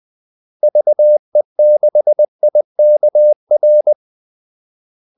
Morse code, heard as a keyed tone, about 20 wpm, 600 Hz. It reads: VE6IKR
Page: Kaldesignaler 20 wpm